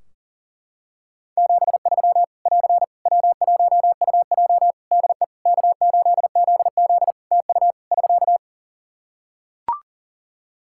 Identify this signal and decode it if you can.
Morse code, heard as a keyed tone, about 37 wpm, 700 Hz. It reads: 73 <AR> W1UJ DE K8ZZ TU <SK>
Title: Practice QSO - Exchange 4 - Ending - 40wpm